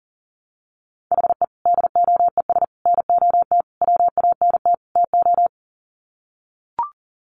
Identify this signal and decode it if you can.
Morse code, heard as a keyed tone, about 40 wpm, 700 Hz. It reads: HE DOES NOT WANT TO
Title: Sets of 5 Words 40wpm